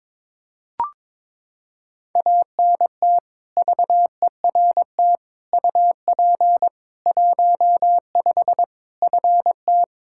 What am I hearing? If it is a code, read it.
Morse code, heard as a keyed tone, about 21 wpm, 700 Hz. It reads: ANT VERT UP 15 FT
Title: Practice QSO - Exchange 2 - Info - 22wpm